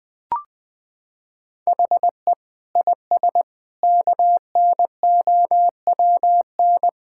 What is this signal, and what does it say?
Morse code, heard as a keyed tone, about 19 wpm, 700 Hz. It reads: HE IS KNOWN
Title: Sentences with 4-Word Phrases - 20wpm